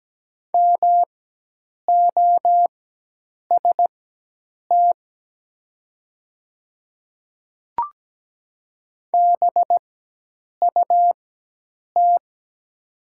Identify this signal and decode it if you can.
Morse code, heard as a keyed tone, about 17 wpm, spaced as for 8 wpm, 700 Hz. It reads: MOST BUT
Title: 4x Character Spacing - Top 200 Words - 17wpm